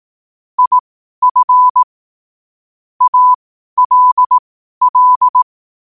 Morse code, heard as a keyed tone, about 18 wpm, spaced as for 12 wpm, 1000 Hz. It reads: IF ALL
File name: qotdfiles / 12wpm